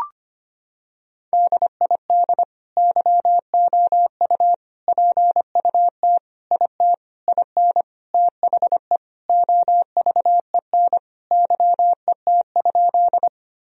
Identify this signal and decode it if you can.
Morse code, heard as a keyed tone, about 25 wpm, 700 Hz. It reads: DID YOU PUT IT IN THE OVEN YET?
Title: Sentences from Top 1000 Words - Review Part 4 - 25wpm